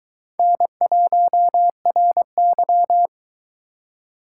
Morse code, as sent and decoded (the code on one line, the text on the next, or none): -. .---- .-. -.--
N1RY